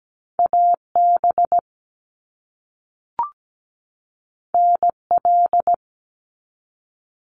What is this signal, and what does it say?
Morse code, heard as a keyed tone, about 17 wpm, 700 Hz. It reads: AB NL
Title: Canadian Provinces and Territories 17wpm